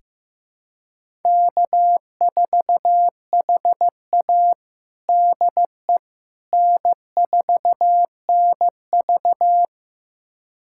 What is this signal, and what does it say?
Morse code, heard as a keyed tone, about 15 wpm, 700 Hz. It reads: K4HA DE N4NV
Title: USA Callsign DE Callsign - 4 Characters - 15wpm